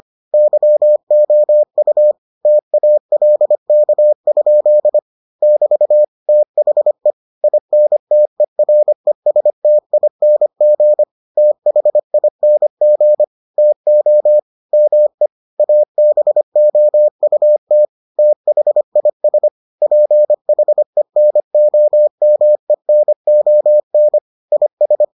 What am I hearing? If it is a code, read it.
Morse code, heard as a keyed tone, about 25 wpm, 600 Hz. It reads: YOU TALK? <BT> THE INTERESTING THING TO ME ABOUT THIS PHENOMENON IS